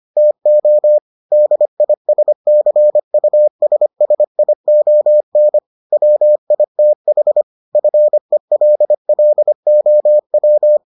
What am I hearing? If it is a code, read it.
Morse code, heard as a keyed tone, about 25 wpm, 600 Hz. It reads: TO DISCUSSION WITH FELLOW